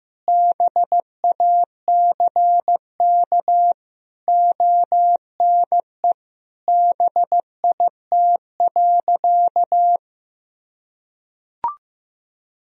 Morse code, heard as a keyed tone, about 15 wpm, 700 Hz. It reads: BACK ONE BIT.
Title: Sentences from Top 1000 Words - Review Part 1 - 15wpm